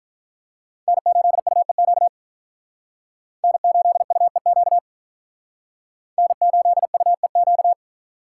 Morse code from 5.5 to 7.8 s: N8UEQ

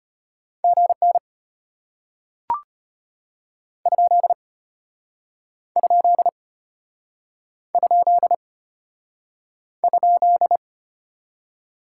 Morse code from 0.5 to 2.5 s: GN

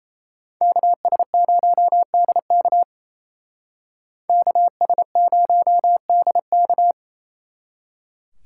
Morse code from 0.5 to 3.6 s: KS0DK

